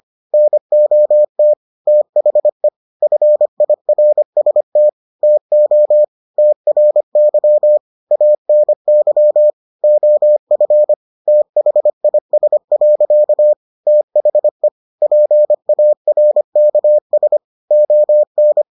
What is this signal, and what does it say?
Morse code, heard as a keyed tone, about 25 wpm, 600 Hz. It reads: NOT THE FIRST TO TRY ANY OF THIS. THE PARKS ON